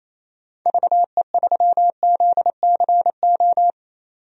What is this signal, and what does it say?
Morse code, heard as a keyed tone, about 28 wpm, 700 Hz. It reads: VE3ZCO
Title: DXCC-100 Call Signs - 6 Characters - 28wpm